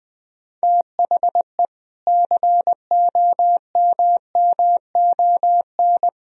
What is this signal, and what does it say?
Morse code, heard as a keyed tone, about 20 wpm, 700 Hz. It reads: THE COMMON
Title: Sentences from Top 1000 Words - Review Part 3 - 20wpm